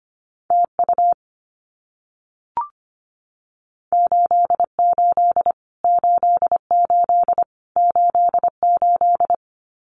Morse code, heard as a keyed tone, about 25 wpm, 700 Hz. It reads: TU 88 88 88
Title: Abbreviations Repeated 3x - 25wpm